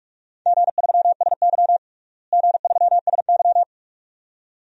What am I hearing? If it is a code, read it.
Morse code, heard as a keyed tone, about 46 wpm, 700 Hz. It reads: G3SY G3SY
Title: DXCC-100 Call Signs - 4 Characters - Repeated 2x - 45wpm